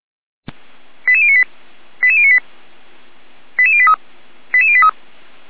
Five-tone selective call (ZVEI), heard as quick runs of five tones, tones of about 70 ms. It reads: 88098 88098 88083 88083
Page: SDR/ZVEI